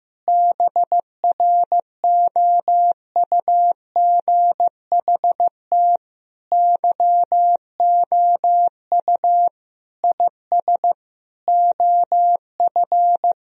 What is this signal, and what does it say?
Morse code, heard as a keyed tone, about 15 wpm, 700 Hz. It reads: BROUGHT YOU IS OF
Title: Sentences from Top 1000 Words - Review Part 3 - 15wpm